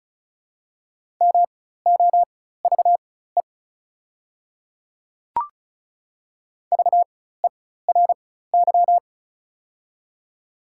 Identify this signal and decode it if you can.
Morse code, heard as a keyed tone, about 34 wpm, spaced as for 16 wpm, 690 Hz. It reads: MOVE VERY
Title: 4x Character Spacing - Top 200 Words - 35wpm